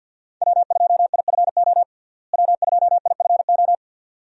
Morse code, heard as a keyed tone, about 48 wpm, 700 Hz. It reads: W2IFO W2IFO
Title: USA Call Signs - 5 Characters - Repeated 2x - 50wpm